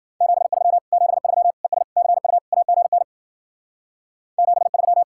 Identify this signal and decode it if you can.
Morse code, heard as a keyed tone, about 51 wpm, 700 Hz. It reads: 73 73 ES CU AGN 73